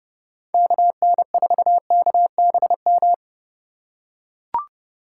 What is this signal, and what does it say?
Morse code, heard as a keyed tone, about 31 wpm, 700 Hz. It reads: KN4KBM
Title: USA Call Signs - 6 Characters - 4x Character Spacing - 30wpm